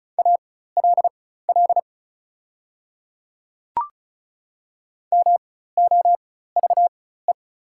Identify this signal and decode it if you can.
Morse code, heard as a keyed tone, about 36 wpm, spaced as for 16 wpm, 690 Hz. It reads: ALL MOVE
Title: 4x Character Spacing - Top 200 Words - 35wpm